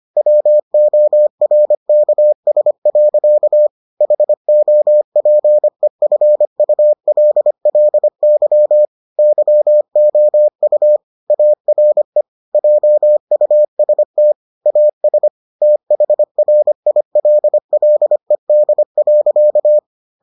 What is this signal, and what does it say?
Morse code, heard as a keyed tone, about 25 wpm, 600 Hz. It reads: WORKS. HOPEFULLY YOU ARE JUST AS THRILLED.